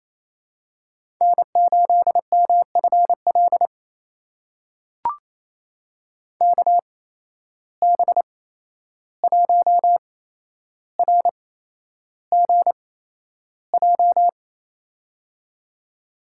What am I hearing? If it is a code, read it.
Morse code, heard as a keyed tone, about 28 wpm, spaced as for 11 wpm, 700 Hz. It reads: N8MFL KB1RGJ
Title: USA Call Signs - 8x Character Spacing - 28wpm